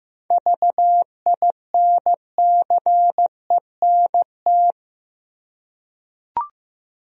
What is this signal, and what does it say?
Morse code, heard as a keyed tone, about 15 wpm, 700 Hz. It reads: VINCENT
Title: USA Common Names - 15wpm